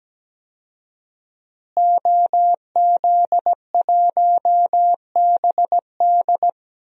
Morse code, heard as a keyed tone, about 17 wpm, 700 Hz. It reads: OZ1BD
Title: DXCC-100 Call Signs - 5 Characters - Repeated 2x - 17wpm